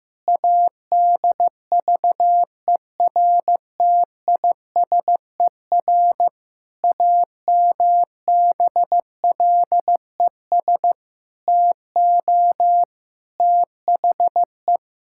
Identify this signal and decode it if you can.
Morse code, heard as a keyed tone, about 15 wpm, 700 Hz. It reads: ADVERTISER AMBLES TO THE
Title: Pangram Challenge v2 - 15wpm